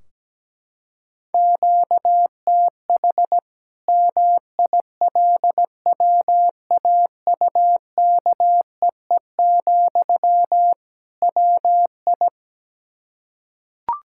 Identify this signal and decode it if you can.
Morse code, heard as a keyed tone, about 17 wpm, 700 Hz. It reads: QTH MILWAUKEE, WI
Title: Practice QSO - Exchange 1 - Beginning - 17wpm